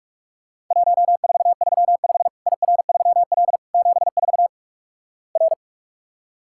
Morse code, heard as a keyed tone, about 46 wpm, 700 Hz. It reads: 1435 IR3L 74 R
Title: Worked All Europe DX Contest - QTC Exchange - 45wpm